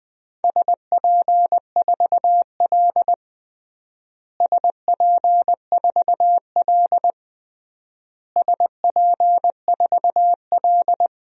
Morse code, heard as a keyed tone, about 21 wpm, 700 Hz. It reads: SP4L SP4L SP4L
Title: DXCC-100 Call Signs - 4 Characters - Repeated 3x - 20wpm